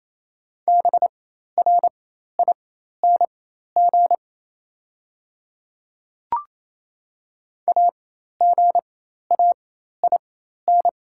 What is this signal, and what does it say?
Morse code, heard as a keyed tone, about 28 wpm, spaced as for 13 wpm, 700 Hz. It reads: BRING AGAIN
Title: 4x Character Spacing - Top 500 Words - 28wpm